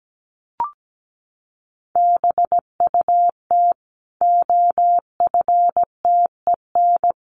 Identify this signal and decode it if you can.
Morse code, heard as a keyed tone, about 17 wpm, 700 Hz. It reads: BUT OFTEN
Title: Sentences from Top 1400 Words 17wpm